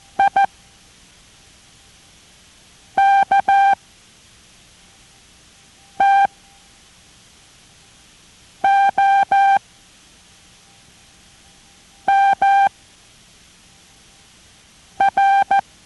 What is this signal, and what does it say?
Morse code, heard as a keyed tone, about 14 wpm, spaced as for 3 wpm, 790 Hz. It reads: IKTOMR